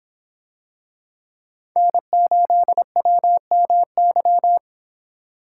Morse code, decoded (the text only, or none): N8WMY